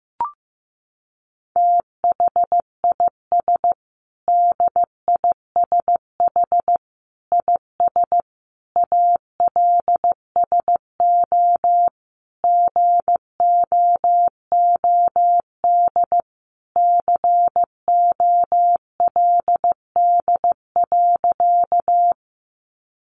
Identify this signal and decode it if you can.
Morse code, heard as a keyed tone, about 15 wpm, 700 Hz. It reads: THIS DISH IS ALSO GOOD COLD.
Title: Sentences from Top 1300 Words 15wpm